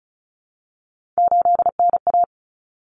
Morse code, heard as a keyed tone, about 34 wpm, 700 Hz. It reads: 8NA
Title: Common Cut Numbers - 3-Digit 35wpm